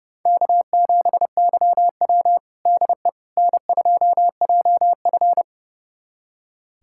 Morse code, heard as a keyed tone, about 30 wpm, 700 Hz. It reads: K7YW DE N2JF
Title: USA Callsign DE Callsign - 4 Characters - 30wpm